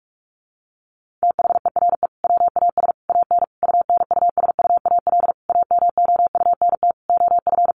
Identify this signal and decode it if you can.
Morse code, heard as a keyed tone, about 44 wpm, 700 Hz. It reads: THERE WAS AN UNUSUAL AMOUNT OF
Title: Sentences from Top 1400 Words 45wpm